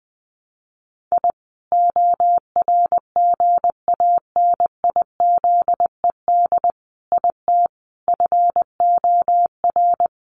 Morse code, heard as a keyed tone, about 20 wpm, 700 Hz. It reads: I ORGANIZED IT FOR
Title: Sentences from Top 1300 Words 20wpm